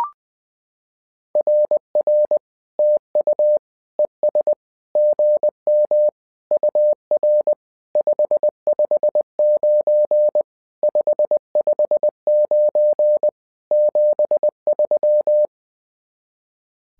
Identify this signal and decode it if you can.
Morse code, heard as a keyed tone, about 20 wpm, 600 Hz. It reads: RR TU ES GM UR 559 559 73
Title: SOTA QSO Practice 20wpm